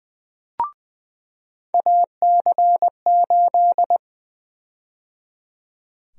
Morse code, heard as a keyed tone, about 20 wpm, 700 Hz. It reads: AC8
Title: USA Call Sign Prefixes - 20wpm